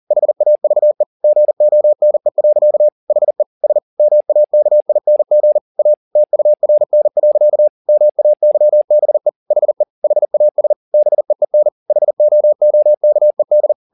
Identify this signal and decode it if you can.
Morse code, heard as a keyed tone, about 40 wpm, 600 Hz. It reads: HAVE GONE. HE S MAKING A TURN. MAYBE HE HAS BEEN HOOKED